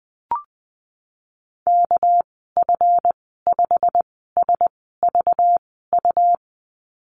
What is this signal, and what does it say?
Morse code, heard as a keyed tone, about 19 wpm, spaced as for 14 wpm, 700 Hz. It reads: KF5SVU
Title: 200 Lesson 32g - Letter k - Call Signs 20wpm